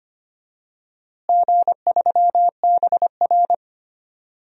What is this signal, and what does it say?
Morse code, heard as a keyed tone, about 25 wpm, 700 Hz. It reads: G3BR
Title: DXCC-100 Call Signs - 4 Characters - Repeated 3x - 25wpm